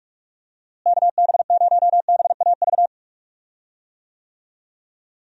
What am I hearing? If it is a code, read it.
Morse code, heard as a keyed tone, about 42 wpm, 700 Hz. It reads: KB0BAV
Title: USA Call Signs - 6 Characters - Repeated 2x - 45wpm